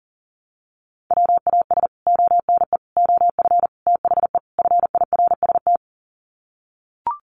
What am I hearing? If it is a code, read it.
Morse code, heard as a keyed tone, about 40 wpm, 700 Hz. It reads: WAS ONE OF THE FIRST
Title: Sets of 5 Words 40wpm